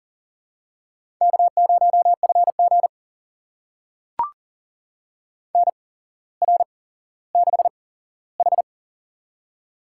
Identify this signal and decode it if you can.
Morse code, heard as a keyed tone, about 43 wpm, spaced as for 17 wpm, 700 Hz. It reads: K0FG NR6H
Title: USA Call Signs - 4 Characters - 8x Character Spacing - 40wpm